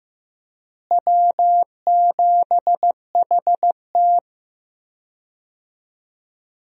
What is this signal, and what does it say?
Morse code, heard as a keyed tone, about 15 wpm, 700 Hz. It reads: W7HT
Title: USA Call Signs - 4 Characters - Repeated 3x - 15wpm